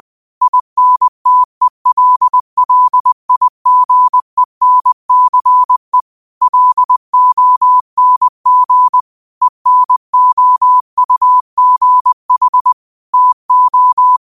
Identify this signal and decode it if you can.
Morse code, heard as a keyed tone, about 20 wpm, 1000 Hz. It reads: INTELLIGENCE LONG ENOUGH TO